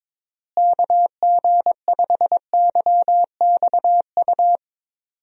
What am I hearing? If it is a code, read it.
Morse code, heard as a keyed tone, about 22 wpm, 700 Hz. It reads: KG5YXU
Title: USA Call Signs - 6 Characters - Repeated 3x - 22wpm